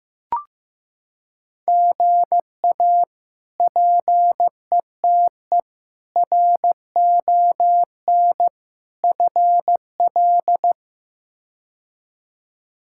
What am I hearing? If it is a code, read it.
Morse code, heard as a keyed tone, about 15 wpm, 700 Hz. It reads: GA PETE RON FL
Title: K1USN SST QSO Practice - 15wpm